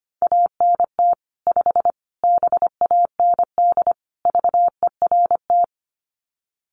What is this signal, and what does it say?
Morse code, heard as a keyed tone, about 25 wpm, 700 Hz. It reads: ANT 5 BAND VERT
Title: Common QSO Phrases 25wpm